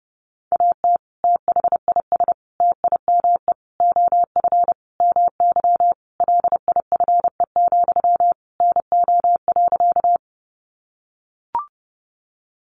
Morse code, heard as a keyed tone, about 30 wpm, 700 Hz. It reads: AT THIS TIME OF MY LIFE, NO.
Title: Sentences from Top 300 Words 30wpm